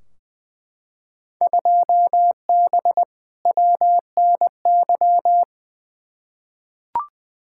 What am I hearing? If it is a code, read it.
Morse code, heard as a keyed tone, about 20 wpm, 700 Hz. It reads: 2B WNY
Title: Field Day Exchanges Encore - 20wpm